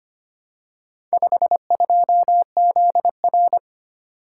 Morse code, heard as a keyed tone, about 25 wpm, 700 Hz. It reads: 52ZR
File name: practice / 25wpm / Four Letter-Number Combinations - 25wpm